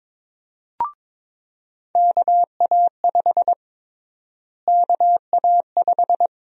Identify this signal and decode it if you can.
Morse code, heard as a keyed tone, about 23 wpm, 700 Hz. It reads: KA5 KA5
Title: USA Call Sign Prefixes - Repeated 2x - 22wpm